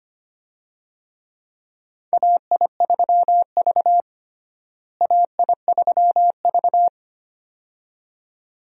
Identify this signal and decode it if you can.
Morse code, heard as a keyed tone, about 25 wpm, 700 Hz. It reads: AI3V AI3V
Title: USA Call Signs - 4 Characters - Repeated 2x - 25wpm